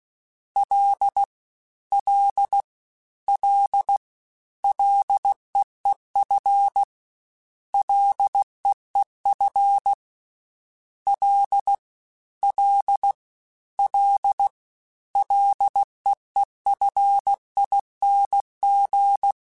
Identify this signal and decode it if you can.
Morse code, heard as a keyed tone, about 16 wpm, 800 Hz. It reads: L L L LEEF LEEF L L L LEEFING
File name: Ls / Single L m3s / Single L Prefixe 3x2 Twice @15